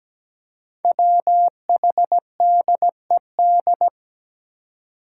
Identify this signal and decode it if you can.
Morse code, heard as a keyed tone, about 17 wpm, 700 Hz. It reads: WHDED
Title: Five Challenging Letter-Number Combinations 17wpm